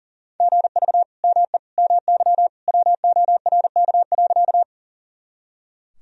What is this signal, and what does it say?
Morse code, heard as a keyed tone, about 40 wpm, 700 Hz. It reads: GV ME MY WORK.
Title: Sentences from Top 200 Words and Abbreviations 40wpm